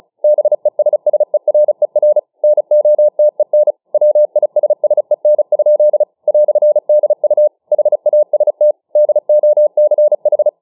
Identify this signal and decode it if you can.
Morse code, heard as a keyed tone, about 34 wpm, 600 Hz. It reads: BESSERER NOTEN WISSEN? "DU HAST DOCH
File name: dk5ke / Morsemogeln 35WpM